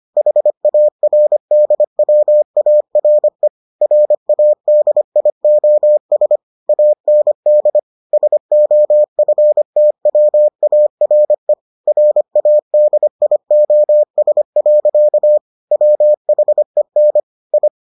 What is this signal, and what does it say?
Morse code, heard as a keyed tone, about 25 wpm, 600 Hz. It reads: HARDWARE RADIOS AND SOFTWARE RADIOS. WHEN I